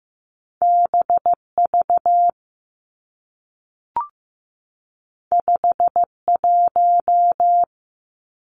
Morse code, heard as a keyed tone, about 15 wpm, 700 Hz. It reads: BV 51
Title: Two Challenging Letter-Number Combinations 15wpm